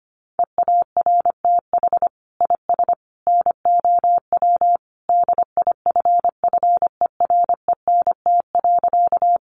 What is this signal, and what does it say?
Morse code, heard as a keyed tone, about 25 wpm, 700 Hz. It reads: EARTH IS NOW DIFFERENT.
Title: Sentences from Top 600 Words 25wpm